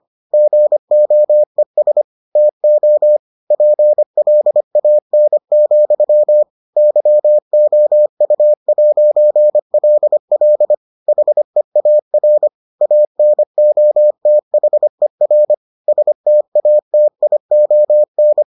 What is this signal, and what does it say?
Morse code, heard as a keyed tone, about 25 wpm, 600 Hz. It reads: GOES TO PLAN, YOU'LL HEAR ANOTHER STATION